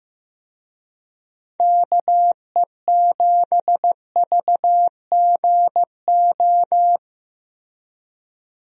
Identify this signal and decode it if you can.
Morse code, heard as a keyed tone, about 15 wpm, 700 Hz. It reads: KE7VGO
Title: USA Call Signs - 6 Characters - Repeated 3x - 15wpm